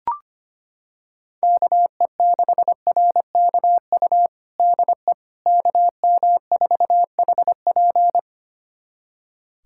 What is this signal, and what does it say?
Morse code, heard as a keyed tone, about 25 wpm, 700 Hz. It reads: KE6RKU DE KM4HP
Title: USA Callsign DE Callsign 25wpm